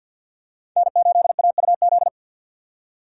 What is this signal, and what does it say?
Morse code, heard as a keyed tone, about 52 wpm, 700 Hz. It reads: N8AUZ
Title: USA Call Signs - 5 Characters - 4x Character Spacing - 50wpm